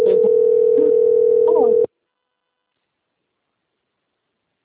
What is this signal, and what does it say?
Ringback tone (North America), 440 + 480 Hz.
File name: out- - - - - .22433